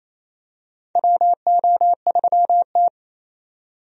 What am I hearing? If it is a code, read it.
Morse code, heard as a keyed tone, about 28 wpm, 700 Hz. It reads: WO3T
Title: USA Call Signs - 4 Characters - 28wpm